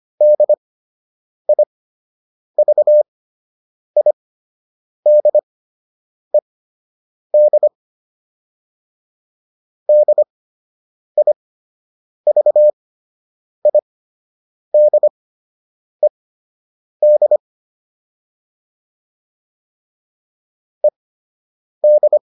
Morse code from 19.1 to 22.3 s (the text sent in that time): ED